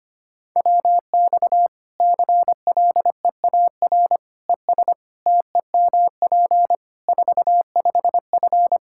Morse code, heard as a keyed tone, about 25 wpm, 700 Hz. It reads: WX CLEAR ES TEMP 45F
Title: Practice QSO - Exchange 2 - Info - 25wpm